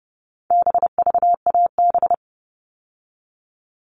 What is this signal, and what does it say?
Morse code, heard as a keyed tone, about 30 wpm, 700 Hz. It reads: BVAB
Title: All Cut Numbers - 4-Digit 30wpm